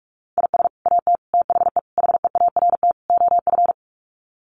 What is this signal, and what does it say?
Morse code, heard as a keyed tone, about 44 wpm, 700 Hz. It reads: IS AT THE HEART OF
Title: Sets of 5 Words Encore2 45wpm